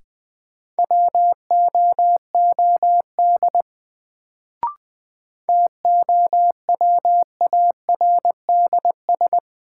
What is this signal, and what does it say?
Morse code, heard as a keyed tone, about 20 wpm, 700 Hz. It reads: WOOD TOWARDS
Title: Sentences from Top 1000 Words - Review Part 2 - New Word Drill - 20wpm